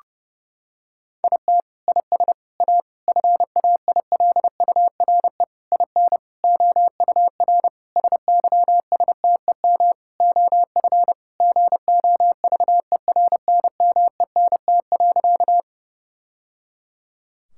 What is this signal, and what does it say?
Morse code, heard as a keyed tone, about 30 wpm, 700 Hz. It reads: IT IS A FAILURE IN OUR SYSTEM OF GOVERNMENT.
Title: Sentences from Top 1000 Words - Review Part 6 - 30wpm